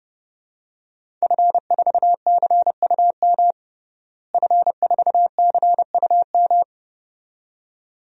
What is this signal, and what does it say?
Morse code, heard as a keyed tone, about 32 wpm, 700 Hz. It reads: F4CUM F4CUM
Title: DXCC-100 Call Signs - 5 Characters - Repeated 2x - 30wpm